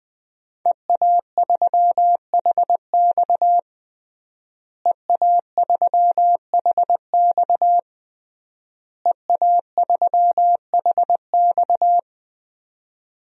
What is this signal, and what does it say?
Morse code, heard as a keyed tone, about 20 wpm, 700 Hz. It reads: EA3HX EA3HX EA3HX
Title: DXCC-100 Call Signs - 5 Characters - Repeated 3x - 20wpm